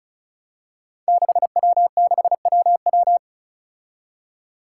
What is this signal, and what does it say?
Morse code, heard as a keyed tone, about 36 wpm, 700 Hz. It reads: 6W6WW
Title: Five Challenging Letter-Number Combinations 35wpm